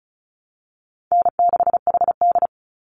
Morse code, heard as a keyed tone, about 34 wpm, 700 Hz. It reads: N6HD
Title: Easy Call Signs Repeated 3x - 35wpm